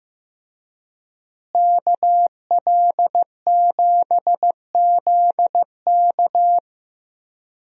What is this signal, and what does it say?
Morse code, heard as a keyed tone, about 15 wpm, 700 Hz. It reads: KL7ZK
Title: USA Call Signs - 5 Characters - Repeated 3x - 15wpm